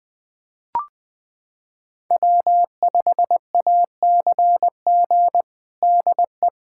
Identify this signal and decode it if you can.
Morse code, heard as a keyed tone, about 20 wpm, 700 Hz. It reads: W5ACG DE
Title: USA Callsign DE Callsign - 5 Characters - 20wpm